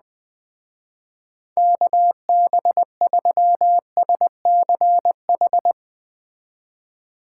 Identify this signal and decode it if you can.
Morse code, heard as a keyed tone, about 20 wpm, 700 Hz. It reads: KB3SCH
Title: Speed Racing - USA 2x3 Call Signs 17wpm